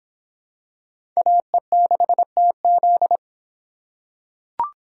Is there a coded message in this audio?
Yes: Morse code from 1.2 to 3.2 s, heard as a keyed tone, about 23 wpm, 700 Hz. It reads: AE6TZ